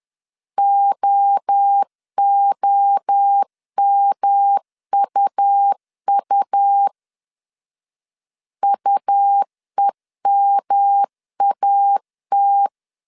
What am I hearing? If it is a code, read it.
Morse code, heard as a keyed tone, about 10 wpm, 800 Hz. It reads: OOMUU UEMAT
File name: CW / DET / 1 3 1500C S B 10wpm 3335